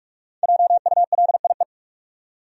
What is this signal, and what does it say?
Morse code, heard as a keyed tone, about 48 wpm, 700 Hz. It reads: JULIE
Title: USA Common Names - Max 5 Letters - 45wpm